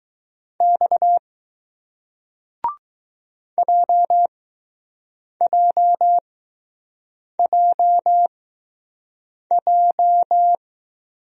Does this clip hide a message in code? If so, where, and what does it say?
Yes: Morse code from 0.6 to 10.6 s, heard as a keyed tone, about 20 wpm, spaced as for 6 wpm, 700 Hz. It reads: X JJJJ